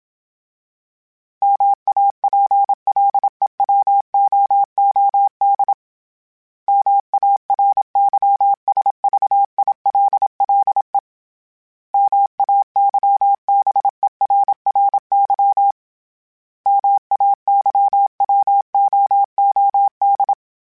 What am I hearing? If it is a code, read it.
Morse code, heard as a keyed tone, about 27 wpm, 800 Hz. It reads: MAPLEWOOD MARYSVILLE MAYBERRY MAYWOOD
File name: fromSMs / FromSMs mp3s / Cities@25